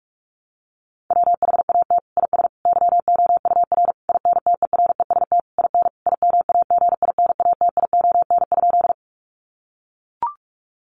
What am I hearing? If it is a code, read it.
Morse code, heard as a keyed tone, about 44 wpm, 700 Hz. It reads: WHAT IS YOUR INTEREST IN IMAGINATION?
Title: Sentences from Top 1300 Words 45wpm